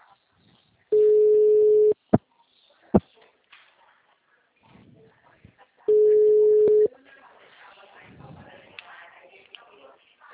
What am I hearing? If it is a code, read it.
Ringback tone (Europe, 425 Hz).
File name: out- -2012- - - .5938